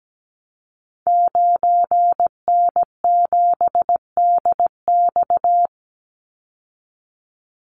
Morse code, heard as a keyed tone, about 17 wpm, 700 Hz. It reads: 9N7DX
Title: Call Signs Repeated 3x - 17wpm